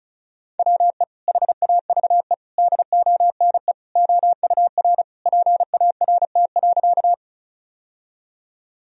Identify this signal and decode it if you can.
Morse code, heard as a keyed tone, about 34 wpm, 700 Hz. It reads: WE HAVE DONE OUR PART.
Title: Sentences from Top 1000 Words - Review Part 1 - 35wpm